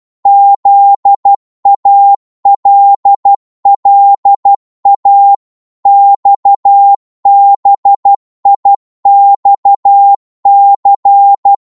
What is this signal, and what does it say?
Morse code, heard as a keyed tone, about 12 wpm, 800 Hz. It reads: ZALLA XBIXC
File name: cwmp3 / ltr-12wpm